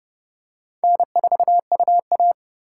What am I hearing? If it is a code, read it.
Morse code, heard as a keyed tone, about 30 wpm, 700 Hz. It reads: N4UA
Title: USA Call Signs - 2x Character Spacing - 30wpm